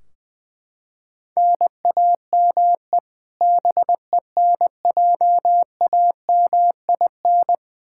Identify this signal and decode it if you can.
Morse code, heard as a keyed tone, about 20 wpm, 700 Hz. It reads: NAME BENJAMIN
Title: K1USN SST QSO Practice 20wpm